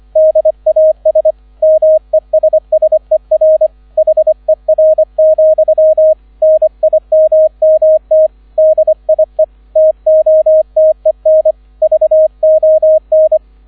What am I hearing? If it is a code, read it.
Morse code, heard as a keyed tone, about 25 wpm, 630 Hz, with radio noise. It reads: DAS MESSER HER, NIMMT DIE TOTEN VON